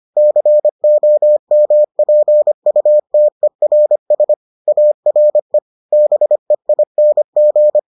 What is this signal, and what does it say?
Morse code, heard as a keyed tone, about 25 wpm, 600 Hz. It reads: COMPUTERS ARE BEING